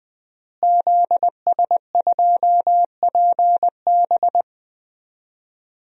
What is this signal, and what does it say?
Morse code, heard as a keyed tone, about 20 wpm, 700 Hz. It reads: ZS2PB
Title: DXCC-100 Call Signs - 5 Characters - Repeated 2x - 20wpm